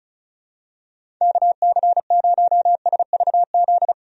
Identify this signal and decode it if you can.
Morse code, heard as a keyed tone, about 36 wpm, 700 Hz. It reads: KC0SVZ
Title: USA Call Signs - 6 Characters - Repeated 3x - 35wpm